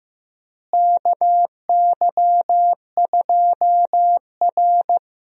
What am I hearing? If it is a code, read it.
Morse code, heard as a keyed tone, about 16 wpm, 700 Hz. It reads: KY2R
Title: USA Call Signs - 4 Characters - Repeated 3x - 15wpm